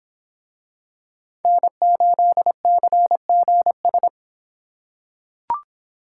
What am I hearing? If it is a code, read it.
Morse code, heard as a keyed tone, about 24 wpm, 700 Hz. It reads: N8CGS